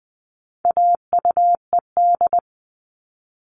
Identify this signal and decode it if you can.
Morse code, heard as a keyed tone, about 20 wpm, 700 Hz. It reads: AUED
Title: All Cut Numbers - 4-Digit 20wpm